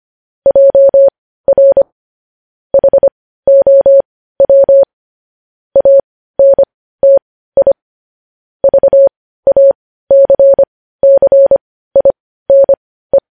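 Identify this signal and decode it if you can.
Morse code, heard as a keyed tone, about 25 wpm, spaced as for 15 wpm, 550 Hz. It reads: JR HOW ANTI VACCINE
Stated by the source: Morse practice sending news headlines at 15 wpm.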